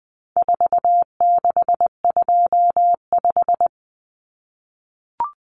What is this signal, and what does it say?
Morse code, heard as a keyed tone, about 20 wpm, 700 Hz. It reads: 4625
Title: Common Cut Numbers - 4-Digit 20wpm